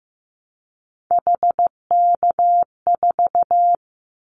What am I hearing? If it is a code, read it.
Morse code, heard as a keyed tone, about 15 wpm, 700 Hz. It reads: HK4
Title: Callsign Prefixes for Top 100 DXCC - 15wpm